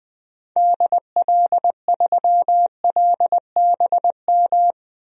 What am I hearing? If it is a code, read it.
Morse code, heard as a keyed tone, about 20 wpm, 700 Hz. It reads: DL3LBM
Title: DXCC-100 Call Signs - 6 Characters - Repeated 3x - 20wpm